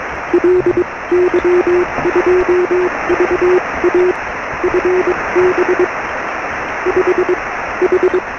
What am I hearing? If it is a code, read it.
Morse code, heard as a keyed tone, about 21 wpm, spaced as for 18 wpm, 400 Hz, with noise. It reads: LY2VA FB 5 H
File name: sm7lcb / 0432 ly2va 13cm